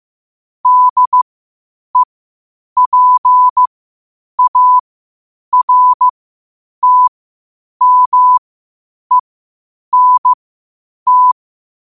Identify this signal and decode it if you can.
Morse code, heard as a keyed tone, about 15 wpm, spaced as for 8 wpm, 1000 Hz. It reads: DEPARTMENT
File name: qotdfiles / 8wpm